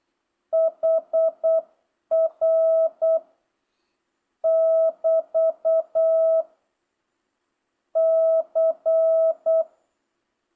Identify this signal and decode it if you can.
Morse code, heard as a keyed tone, about 8 wpm, 650 Hz. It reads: HR <BT> C